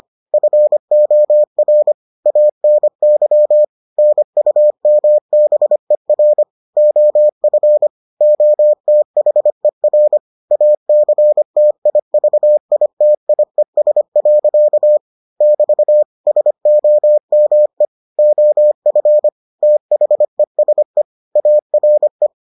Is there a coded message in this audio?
Yes: Morse code from 0.3 to 22.3 s, heard as a keyed tone, about 25 wpm, 600 Hz. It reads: FOR ANY NUMBER OF OTHER ACTIVITIES. <BT> SOME OF THESE ARE